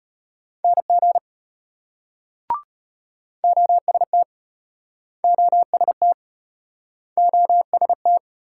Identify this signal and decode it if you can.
Morse code, heard as a keyed tone, about 33 wpm, 700 Hz. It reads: NG OST OST OST